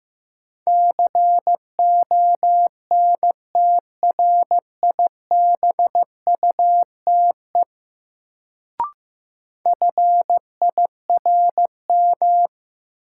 Morse code, heard as a keyed tone, about 16 wpm, 700 Hz. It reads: CONTRIBUTE FIRM
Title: Sentences from Top 1400 Words - New Word Drill - 15wpm